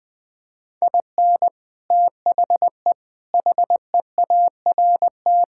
Morse code, heard as a keyed tone, about 20 wpm, 700 Hz. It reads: IN THE HEART
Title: Build A Phrase - 5-Word Phrases - 20wpm